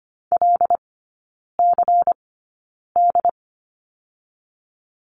Morse code, heard as a keyed tone, about 25 wpm, spaced as for 9 wpm, 700 Hz. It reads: LCD
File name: course / 25:10wpm / 093 Lesson 16e - Letter c - Three Characters 25wpm